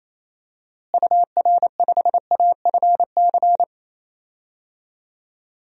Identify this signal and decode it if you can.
Morse code, heard as a keyed tone, about 28 wpm, 700 Hz. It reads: UR5AFC
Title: DXCC-100 Call Signs - 6 Characters - Repeated 2x - 28wpm